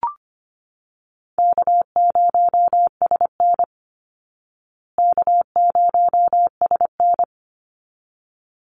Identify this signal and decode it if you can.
Morse code, heard as a keyed tone, about 25 wpm, 700 Hz. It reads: K0SN K0SN
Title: Call Signs Repeated 3x - 25wpm